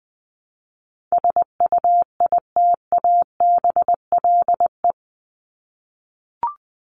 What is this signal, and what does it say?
Morse code, heard as a keyed tone, about 20 wpm, 700 Hz. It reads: SUITABLE
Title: Top 1251-1500 Words 20wpm